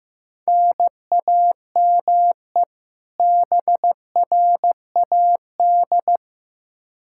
Morse code, heard as a keyed tone, about 15 wpm, 700 Hz. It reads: NAME BRAD
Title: Practice QSO - Exchange 1 - Beginning - 15wpm